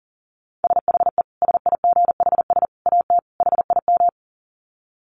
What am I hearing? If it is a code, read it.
Morse code, heard as a keyed tone, about 40 wpm, 700 Hz. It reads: SHE SIGHS AT HIM
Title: Dit Challenge 40wpm